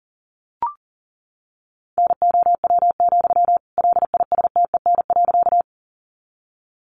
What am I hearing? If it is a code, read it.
Morse code, heard as a keyed tone, about 40 wpm, 700 Hz. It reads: NOW, LISTEN.
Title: Sentences from Top 400 Words 40wpm